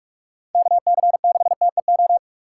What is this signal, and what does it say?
Morse code, heard as a keyed tone, about 47 wpm, 700 Hz. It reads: KC6TEO
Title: USA Call Signs - 4x Character Spacing - 45wpm